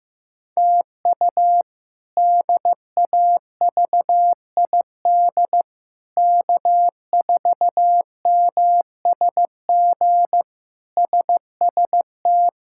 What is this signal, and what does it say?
Morse code, heard as a keyed tone, about 16 wpm, 700 Hz. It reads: TU DAVID K4MSG SST
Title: K1USN SST QSO Practice - 15wpm